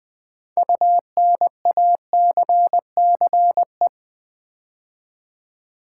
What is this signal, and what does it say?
Morse code, heard as a keyed tone, about 20 wpm, 700 Hz. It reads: UNACCE
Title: Build A Word - 6 Letters At A Time - 12-Letter Words - 20wpm